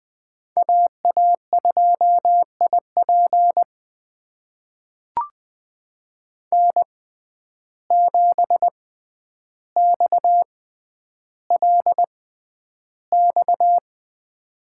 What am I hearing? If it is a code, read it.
Morse code, heard as a keyed tone, about 20 wpm, spaced as for 10 wpm, 700 Hz. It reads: AA2IP N7XLX
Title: USA Call Signs - 5 Characters - 6x Character Spacing - 20wpm